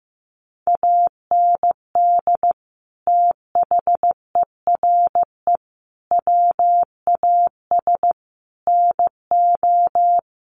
Morse code, heard as a keyed tone, about 15 wpm, 700 Hz. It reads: AND THERE WAS NO
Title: Sentences from Top 1500 Words 15wpm